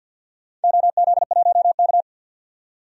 Morse code, heard as a keyed tone, about 49 wpm, 700 Hz. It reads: OZ1K
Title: DXCC-100 Call Signs - 4 Characters - 50wpm